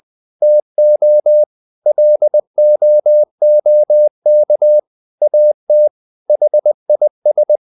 Morse code, heard as a keyed tone, about 20 wpm, 600 Hz. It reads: TO LOOK AT HIS
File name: the old man and the sea part 2 20wpm